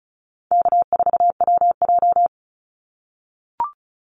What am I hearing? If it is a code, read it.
Morse code, heard as a keyed tone, about 36 wpm, 700 Hz. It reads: K4WJ
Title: Call Signs - 35wpm